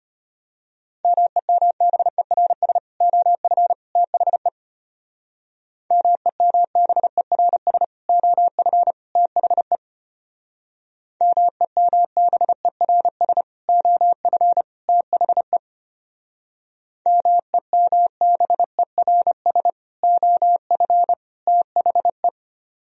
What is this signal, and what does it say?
Morse code, heard as a keyed tone, about 30 wpm, 700 Hz. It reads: MEMBERS OF THE MEMBERS OF THE MEMBERS OF THE MEMBERS OF THE
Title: Speed-Racing - Sets of 3 Words 25wpm